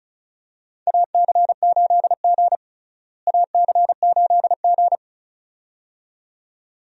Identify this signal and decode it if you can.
Morse code, heard as a keyed tone, about 36 wpm, 700 Hz. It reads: AC8G AC8G
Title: USA Call Signs - 4 Characters - Repeated 2x - 35wpm